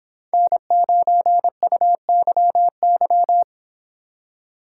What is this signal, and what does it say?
Morse code, heard as a keyed tone, about 29 wpm, 700 Hz. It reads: N9UYY